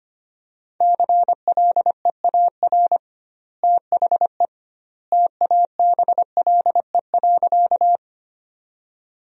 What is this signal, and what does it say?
Morse code, heard as a keyed tone, about 25 wpm, 700 Hz. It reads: CLEAR THE TABLE.
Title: 2x Word-Spacing - Sentences from Top 500 Words - 25wpm